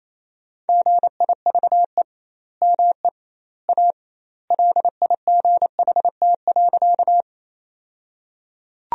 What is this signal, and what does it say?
Morse code, heard as a keyed tone, about 28 wpm, 700 Hz. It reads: GIVE ME A LIGHT.
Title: 2x Word-Spacing - Sentences from Top 200 Words - 28wpm